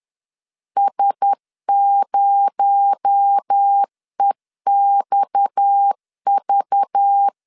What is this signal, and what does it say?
Morse code, heard as a keyed tone, about 10 wpm, 800 Hz. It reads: S0EXV
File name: LEZIONI / 6 PKX0 / 6 PKX0 10wpm 2707